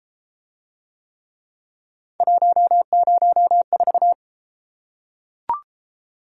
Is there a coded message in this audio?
Yes: Morse code from 2.2 to 4.1 s, heard as a keyed tone, about 31 wpm, 700 Hz. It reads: 104